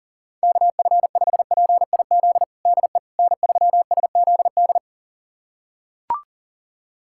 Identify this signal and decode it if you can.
Morse code, heard as a keyed tone, about 40 wpm, 700 Hz. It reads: KF5PIZ DE N3SZD
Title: USA Callsign DE Callsign 40wpm